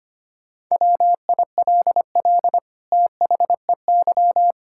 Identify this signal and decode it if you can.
Morse code, heard as a keyed tone, about 25 wpm, 700 Hz. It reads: WILL THEY
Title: Sentences from Top 1000 Words - Review Part 6 - 25wpm